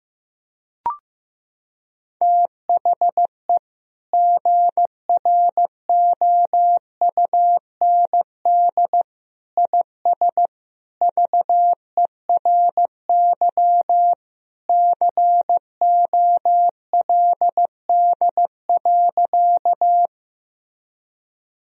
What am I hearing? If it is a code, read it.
Morse code, heard as a keyed tone, about 15 wpm, 700 Hz. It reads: THE GROUND IS VERY COLD.
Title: Sentences from Top 1000 Words - Review Part 2 - 15wpm